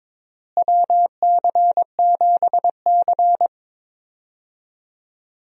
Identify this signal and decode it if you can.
Morse code, heard as a keyed tone, about 22 wpm, 700 Hz. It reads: WC7C
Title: USA Call Signs - 4 Characters - Repeated 3x - 22wpm